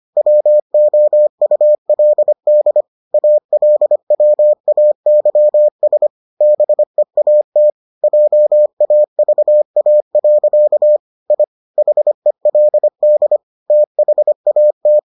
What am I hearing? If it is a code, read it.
Morse code, heard as a keyed tone, about 25 wpm, 600 Hz. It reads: WOULD ALWAYS BEAT JAVA. I HELD THAT